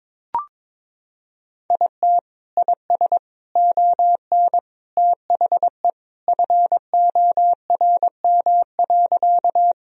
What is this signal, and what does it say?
Morse code, heard as a keyed tone, about 21 wpm, 700 Hz. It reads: IT IS ON THE FORM.
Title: Sentences from Top 1000 Words - Review Part 1 - 22wpm